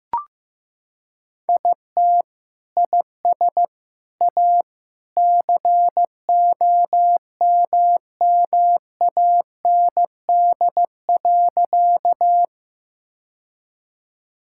Morse code, heard as a keyed tone, about 15 wpm, 700 Hz. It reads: IT IS A COMMAND.
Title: Sentences from Top 1600 Words 15wpm